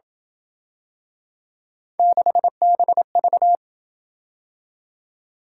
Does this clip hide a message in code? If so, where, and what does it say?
Yes: Morse code from 2.0 to 3.6 s, heard as a keyed tone, about 27 wpm, 700 Hz. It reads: 6BV